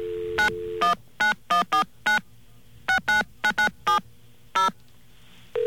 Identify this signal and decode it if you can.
DTMF, heard as telephone touch tones, tones of about 120 ms. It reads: 919149 69997 0